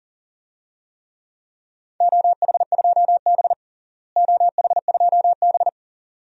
Morse code, heard as a keyed tone, about 40 wpm, 690 Hz. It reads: OH2B OH2B
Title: DXCC-100 Call Signs - 4 Characters - Repeated 2x - 40wpm